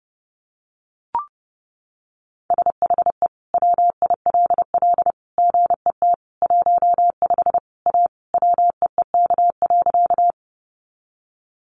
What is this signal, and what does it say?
Morse code, heard as a keyed tone, about 30 wpm, 700 Hz. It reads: SHE WILL GET 15 A WEEK.
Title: Easy Phrases 30wpm